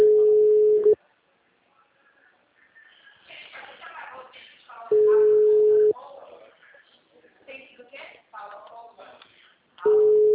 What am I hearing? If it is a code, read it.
Ringback tone (Europe, 425 Hz), with speech.
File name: out- -2005- - - .52714